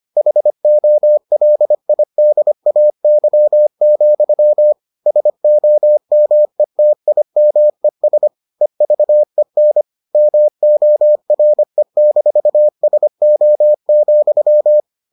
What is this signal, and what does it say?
Morse code, heard as a keyed tone, about 25 wpm, 600 Hz. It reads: HOLIDAY, SOMETIMES EVEN MORE-SO,